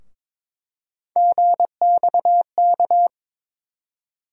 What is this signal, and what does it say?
Morse code, heard as a keyed tone, about 22 wpm, 700 Hz. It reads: GXK
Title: USA Call Sign Postfixes - 3 Letters 22wpm